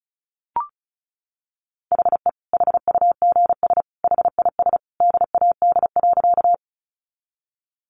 Morse code, heard as a keyed tone, about 34 wpm, 700 Hz. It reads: HE HUGS HIS DAD.
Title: Easy Phrases 35wpm